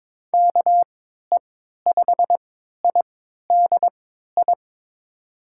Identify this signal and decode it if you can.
Morse code, heard as a keyed tone, about 22 wpm, spaced as for 13 wpm, 700 Hz. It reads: KE5IDI
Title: USA Call Signs - 6 Characters - 3x Character Spacing - 22wpm